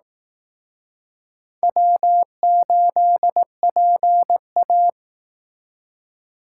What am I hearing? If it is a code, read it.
Morse code, heard as a keyed tone, about 19 wpm, 700 Hz. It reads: W8PA